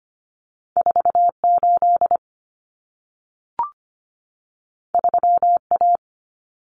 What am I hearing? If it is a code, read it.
Morse code, heard as a keyed tone, about 25 wpm, 700 Hz. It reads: 48 3A
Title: Common Cut Numbers - 2-Digit 25wpm